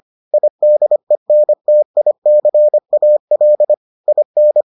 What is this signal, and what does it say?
Morse code, heard as a keyed tone, about 25 wpm, 600 Hz. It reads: IDENTICAL IN